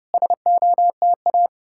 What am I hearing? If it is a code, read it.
Morse code, heard as a keyed tone, about 34 wpm, 700 Hz. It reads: SOTA